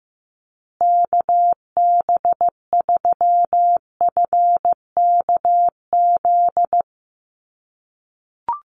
Morse code, heard as a keyed tone, about 15 wpm, 700 Hz. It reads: KB3FKZ
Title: USA Call Signs Repeated 3x 15wpm